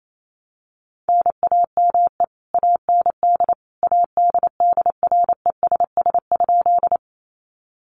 Morse code, heard as a keyed tone, about 28 wpm, 700 Hz. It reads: NAME AND ADDRESS?
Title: Sentences from Top 1000 Words 28wpm